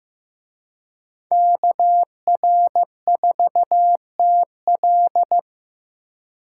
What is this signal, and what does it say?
Morse code, heard as a keyed tone, about 18 wpm, 700 Hz. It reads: KR4TL
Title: Speed Racing - USA Call Signs 15wpm